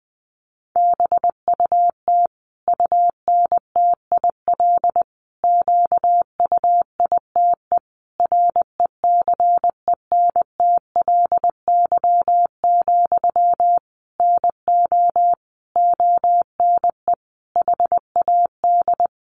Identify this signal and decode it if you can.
Morse code, heard as a keyed tone, about 19 wpm, 700 Hz. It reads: BUT UNTIL QUITE RECENTLY, NO ONE HAD
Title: Sentences from Top 1200 Words 20wpm